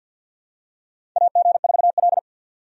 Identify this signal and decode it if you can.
Morse code, heard as a keyed tone, about 52 wpm, 700 Hz. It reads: AG4L
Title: USA Call Signs - 4 Characters - 50wpm